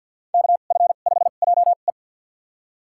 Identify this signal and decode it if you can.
Morse code, heard as a keyed tone, about 49 wpm, spaced as for 36 wpm, 700 Hz. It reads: KF5JE
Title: USA Call Signs - 2x Character Spacing - 50wpm